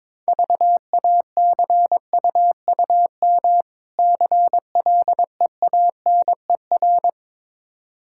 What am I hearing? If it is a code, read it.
Morse code, heard as a keyed tone, about 23 wpm, 690 Hz. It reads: VACUUM CLEANER
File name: Easy Words - Speed-Racing with Hints - 22wpm